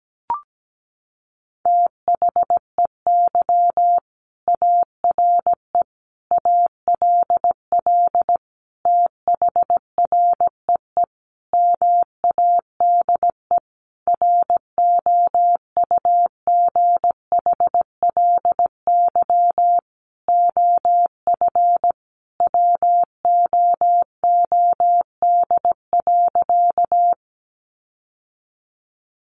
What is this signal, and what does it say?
Morse code, heard as a keyed tone, about 17 wpm, 700 Hz. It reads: THEY ARE ALL THREE MADE ROUGHLY OF WOOD.
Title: Sentences from Top 1500 Words 17wpm